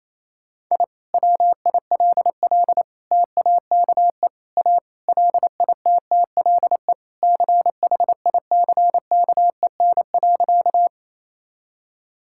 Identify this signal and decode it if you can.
Morse code, heard as a keyed tone, about 28 wpm, 700 Hz. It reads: I WILL TAKE A LITTLE CHICKEN.
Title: Sentences from Top 1000 Words - Review Part 5 - 28wpm